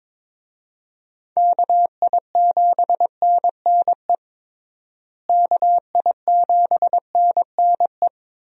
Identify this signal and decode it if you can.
Morse code, heard as a keyed tone, about 22 wpm, 700 Hz. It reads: KI7NNE KI7NNE
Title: USA Call Signs - 6 Characters - Repeated 2x - 22wpm